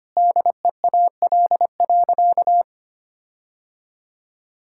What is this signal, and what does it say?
Morse code, heard as a keyed tone, about 25 wpm, 700 Hz. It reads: DEAL.
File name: practice / 25wpm / Sentences with 9-Word Phrases - 25wpm